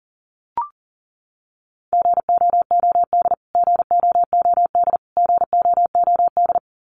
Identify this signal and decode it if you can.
Morse code, heard as a keyed tone, about 40 wpm, 700 Hz. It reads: GOOD GOOD GOOD
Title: Top 300 Words Repeated 40wpm